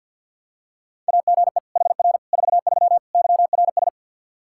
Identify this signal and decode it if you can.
Morse code, heard as a keyed tone, about 48 wpm, 700 Hz. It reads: AGE HR 43 YRS
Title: Practice QSO - Exchange 3 - More Info - 50wpm